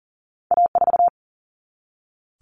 Morse code, heard as a keyed tone, about 40 wpm, 700 Hz. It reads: A4
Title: Common Cut Numbers - 2-Digit 40wpm